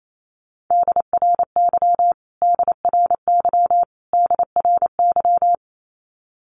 Morse code, heard as a keyed tone, about 28 wpm, 700 Hz. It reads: DRY DRY DRY
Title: Top 500 Words Repeated 28wpm